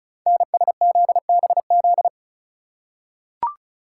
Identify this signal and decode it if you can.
Morse code, heard as a keyed tone, about 36 wpm, 690 Hz. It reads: NSZBZ
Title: Code Groups - 5 Letters - 35wpm